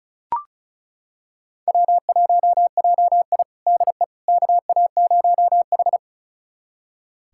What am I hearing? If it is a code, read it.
Morse code, heard as a keyed tone, about 36 wpm, 700 Hz. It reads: W1JI DE KA0H
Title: USA Callsign DE Callsign - 4 Characters - 35wpm